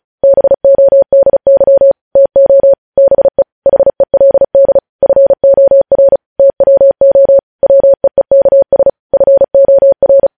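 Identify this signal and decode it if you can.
Morse code, heard as a keyed tone, about 36 wpm, 550 Hz. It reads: BODY TO BE HELD FOR TWO WEEKS FOR
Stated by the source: Morse code news headlines sent at 35 words per minute.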